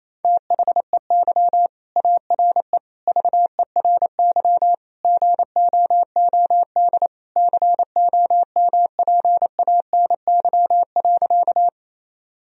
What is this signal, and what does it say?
Morse code, heard as a keyed tone, about 27 wpm, 700 Hz. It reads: THEY ARE VERY GOOD COMPANY.
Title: Sentences from Top 1000 Words - Review Part 3 - 28wpm